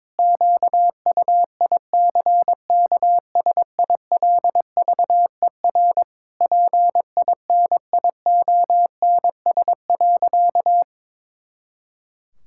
Morse code, heard as a keyed tone, about 22 wpm, 700 Hz. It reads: QUICKSILVER PINIONS.
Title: Pangram Challenge v2 - 22wpm